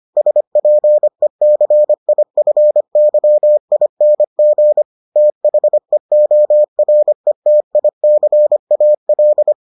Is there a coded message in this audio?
Yes: Morse code from 0.2 to 9.5 s, heard as a keyed tone, about 25 wpm, 600 Hz. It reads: SPECIFYING THEORETICAL